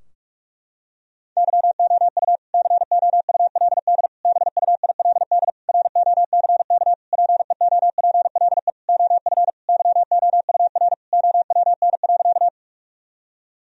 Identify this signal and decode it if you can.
Morse code, heard as a keyed tone, about 46 wpm, 700 Hz. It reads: YOU COULD BUILD ROCK PEOPLE OF YOUR OWN.
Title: Sentences from Top 1000 Words - Review Part 2 - 45wpm